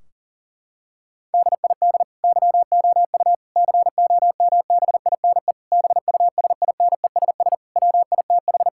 Morse code, heard as a keyed tone, about 37 wpm, 700 Hz. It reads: DID YOU COMBINE BUSINESS WITH
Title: Sentences from Top 1600 Words 40wpm